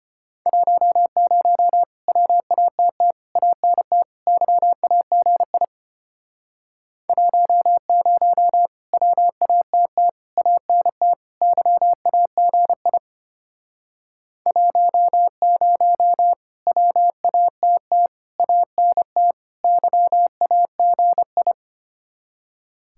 Morse code, heard as a keyed tone, about 33 wpm, 700 Hz. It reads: 10 WATT ANT YAGI 10 WATT ANT YAGI 10 WATT ANT YAGI